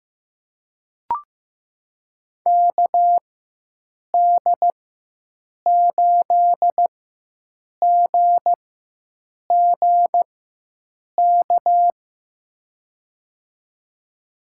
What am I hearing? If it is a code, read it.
Morse code, heard as a keyed tone, about 15 wpm, spaced as for 7 wpm, 700 Hz. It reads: KD8GGK
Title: USA Call Signs - 6 Characters - 4x Character Spacing - 15wpm